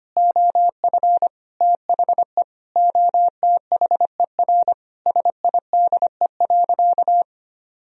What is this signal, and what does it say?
Morse code, heard as a keyed tone, about 25 wpm, 700 Hz. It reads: OF THE OTHER SIDE.
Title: Sentences from Top 1000 Words - Review Part 3 - 25wpm